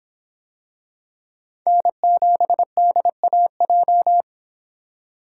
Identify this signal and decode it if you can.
Morse code, heard as a keyed tone, about 24 wpm, 700 Hz. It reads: N7DAJ